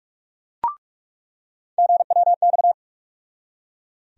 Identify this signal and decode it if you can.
Morse code, heard as a keyed tone, about 42 wpm, 700 Hz. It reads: GWX
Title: USA Call Sign Postfixes - 3 Letters 45wpm